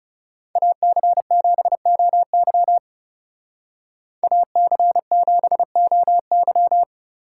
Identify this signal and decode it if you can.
Morse code, heard as a keyed tone, about 39 wpm, 700 Hz. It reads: AC7OY AC7OY